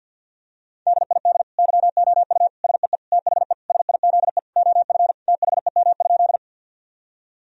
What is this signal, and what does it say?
Morse code, heard as a keyed tone, about 52 wpm, 700 Hz. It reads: DID YOU SEE THE SIZE OF THEM?
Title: Sentences from Top 1000 Words - Review Part 2 - 50wpm